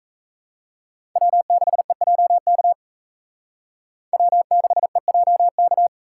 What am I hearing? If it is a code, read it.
Morse code, heard as a keyed tone, about 35 wpm, 700 Hz. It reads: W6EJK W6EJK